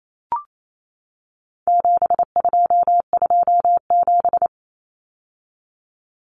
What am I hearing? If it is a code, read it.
Morse code, heard as a keyed tone, about 28 wpm, 700 Hz. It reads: 7227
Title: Common Cut Numbers - 4-Digit 28wpm